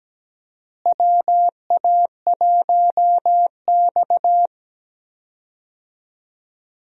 Morse code, heard as a keyed tone, about 17 wpm, 700 Hz. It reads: WA1X
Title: USA Call Signs - 4 Characters - Repeated 2x - 17wpm